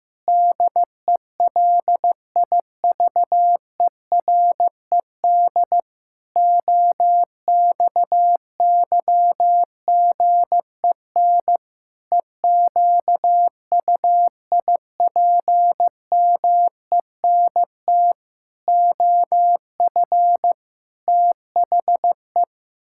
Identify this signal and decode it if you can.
Morse code, heard as a keyed tone, about 16 wpm, 700 Hz. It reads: DELIVERED OXYGEN EQUIPMENT OF THE
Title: Pangram Challenge v2 - 15wpm